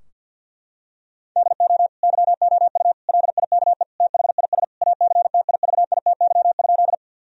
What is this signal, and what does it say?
Morse code, heard as a keyed tone, about 50 wpm, 700 Hz. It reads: DO YOU LIKE THIS ACTIVITY?
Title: Sentences from Top 1000 Words - Review Part 4 - 50wpm